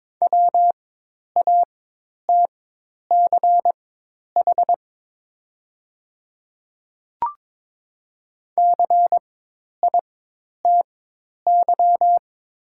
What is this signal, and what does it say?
Morse code, heard as a keyed tone, about 22 wpm, spaced as for 10 wpm, 700 Hz. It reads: WATCH CITY
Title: 4x Character Spacing - Top 500 Words - 22wpm